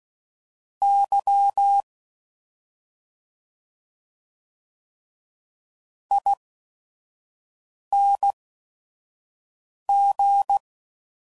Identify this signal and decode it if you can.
Morse code, heard as a keyed tone, about 16 wpm, spaced as for 5 wpm, 800 Hz. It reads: Y ING